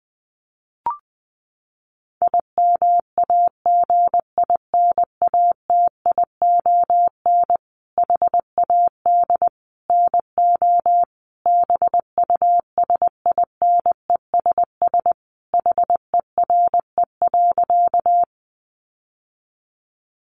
Morse code, heard as a keyed tone, about 20 wpm, 700 Hz. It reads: IMAGINATION HAD NO BUSINESS HERE.
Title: Sentences from Top 1300 Words 20wpm